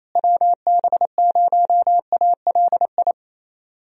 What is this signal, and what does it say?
Morse code, heard as a keyed tone, about 28 wpm, 700 Hz. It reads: WB0ALI
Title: USA Call Signs - 6 Characters - Repeated 2x - 28wpm